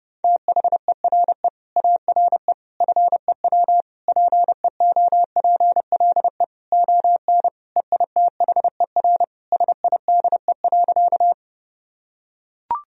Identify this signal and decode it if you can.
Morse code, heard as a keyed tone, about 32 wpm, 700 Hz. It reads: THERE ARE FEW PEOPLE ON EITHER SIDE.
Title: Sentences from Top 1000 Words - Review Part 4 - 30wpm